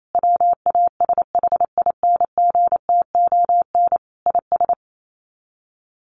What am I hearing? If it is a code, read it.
Morse code, heard as a keyed tone, about 28 wpm, 700 Hz. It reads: WASHINGTON IS
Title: Sets of 5 Words Encore2 28wpm